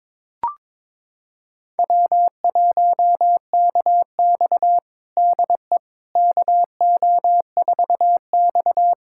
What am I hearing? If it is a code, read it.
Morse code, heard as a keyed tone, about 22 wpm, 700 Hz. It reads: W1KX DE KO4X
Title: USA Callsign DE Callsign - 4 Characters - 22wpm